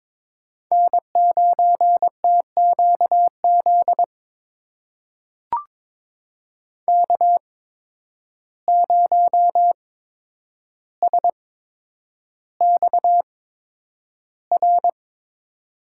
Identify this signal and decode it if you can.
Morse code, heard as a keyed tone, about 22 wpm, spaced as for 9 wpm, 700 Hz. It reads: N9TQZ K0SXR
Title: USA Call Signs - 5 Characters - 8x Character Spacing - 22wpm